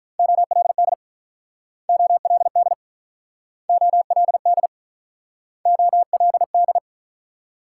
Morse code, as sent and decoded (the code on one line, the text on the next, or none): --- .-.. -.. / --- .-.. -.. / --- .-.. -.. / --- .-.. -..
OLD OLD OLD OLD